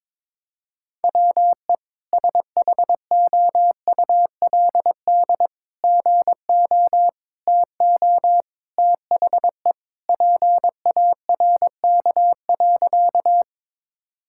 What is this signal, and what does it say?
Morse code, heard as a keyed tone, about 22 wpm, 700 Hz. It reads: WE SHOULD GO TO THE PARK.
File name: practice / 22wpm / Sentences with 3-Word Phrases - 22wpm